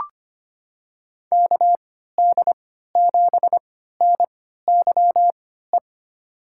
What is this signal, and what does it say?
Morse code, heard as a keyed tone, about 25 wpm, spaced as for 14 wpm, 700 Hz. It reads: KD7NYE
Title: USA Call Signs - 3x Character Spacing - 25wpm